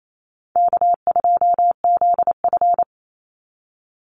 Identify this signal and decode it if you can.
Morse code, heard as a keyed tone, about 28 wpm, 700 Hz. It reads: K2ZF
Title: Easy Call Signs - 28wpm